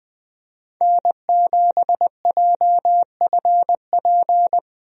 Morse code, heard as a keyed tone, about 20 wpm, 700 Hz. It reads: N7JFP
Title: USA Call Signs - 5 Characters - Repeated 3x - 20wpm